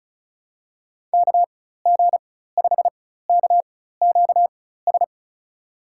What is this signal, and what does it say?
Morse code, heard as a keyed tone, about 36 wpm, spaced as for 16 wpm, 700 Hz. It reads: KG5KQS
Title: USA Call Signs - 6 Characters - 4x Character Spacing - 35wpm